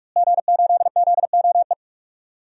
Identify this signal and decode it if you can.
Morse code, heard as a keyed tone, about 43 wpm, 700 Hz. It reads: G8ZOE
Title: DXCC-100 Call Signs - 5 Characters - Repeated 2x - 45wpm